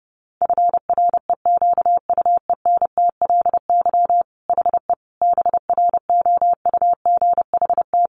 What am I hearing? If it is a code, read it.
Morse code, heard as a keyed tone, about 30 wpm, 700 Hz. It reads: FREQUENTLY HE BROUGHT
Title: Sentences from Top 1300 Words 30wpm